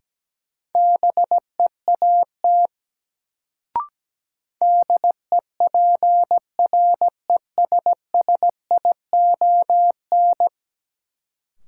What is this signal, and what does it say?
Morse code, heard as a keyed tone, about 17 wpm, 700 Hz. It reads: BEAT DEPRESSION
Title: Sentences from Top 1300 Words - New Word Drill Alt - 17wpm